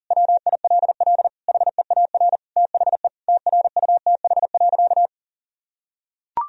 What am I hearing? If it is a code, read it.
Morse code, heard as a keyed tone, about 40 wpm, 700 Hz. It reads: WILL HEAR THE TRUTH.
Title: Sentences from Top 1000 Words - Review Part 1 - 40wpm